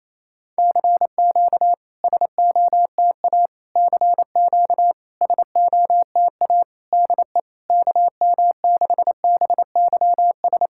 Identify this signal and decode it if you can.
Morse code, heard as a keyed tone, about 28 wpm, 700 Hz. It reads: CQ SOTA CQ SOTA DE KM6BYS
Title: SOTA QSO Practice 28wpm